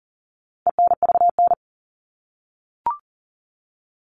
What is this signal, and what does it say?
Morse code, heard as a keyed tone, about 40 wpm, 700 Hz. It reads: ENVN
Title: All Cut Numbers - 4-Digit 40wpm